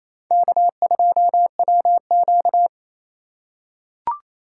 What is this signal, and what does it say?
Morse code, heard as a keyed tone, about 27 wpm, 700 Hz. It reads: K2WQ
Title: USA Call Signs - 4 Characters - 4x Character Spacing - 28wpm